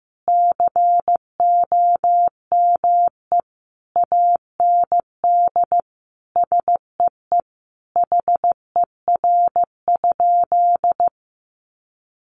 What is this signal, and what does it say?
Morse code, heard as a keyed tone, about 15 wpm, 700 Hz. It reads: COME AND SEE HER?
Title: Sentences from Top 100 Words 15wpm